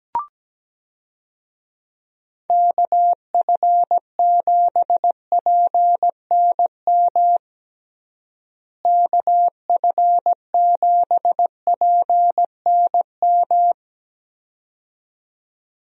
Morse code, heard as a keyed tone, about 17 wpm, 700 Hz. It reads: KF7PNM KF7PNM
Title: USA Call Signs - 6 Characters - Repeated 2x - 17wpm